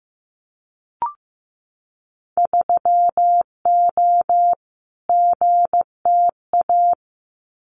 Morse code, heard as a keyed tone, about 15 wpm, 700 Hz. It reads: 3O GTA
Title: Winter Field Day Exchanges Encore 15wpm